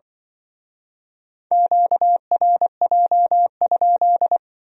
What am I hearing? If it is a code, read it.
Morse code, heard as a keyed tone, about 25 wpm, 700 Hz. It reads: QRJ?
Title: Speed-Racing Q-Codes - 20wpm